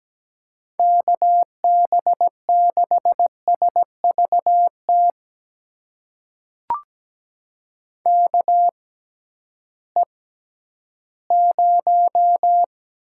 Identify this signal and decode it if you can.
Morse code, heard as a keyed tone, about 17 wpm, spaced as for 11 wpm, 700 Hz. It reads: KB6SVT KE0
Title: USA Call Signs - 6 Characters - 6x Character Spacing - 17wpm